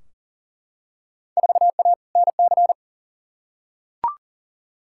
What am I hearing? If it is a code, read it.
Morse code, heard as a keyed tone, about 38 wpm, 700 Hz. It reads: 4A NC
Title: Field Day Exchanges - 40wpm